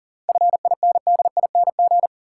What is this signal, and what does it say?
Morse code, heard as a keyed tone, about 40 wpm, 700 Hz. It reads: FINDING
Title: Sentences from Top 1100 Words - New Word Drill Alt - 40wpm